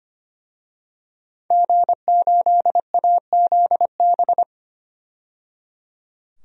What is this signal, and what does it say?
Morse code, heard as a keyed tone, about 25 wpm, 700 Hz. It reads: G8AZB
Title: DXCC-100 Call Signs - 5 Characters - 25wpm